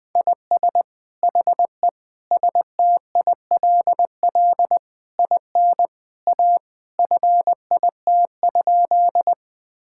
Morse code, heard as a keyed tone, about 20 wpm, 700 Hz. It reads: IS HE STILL IN A FIT?
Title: Sentences from Top 1000 Words - Review Part 5 - 20wpm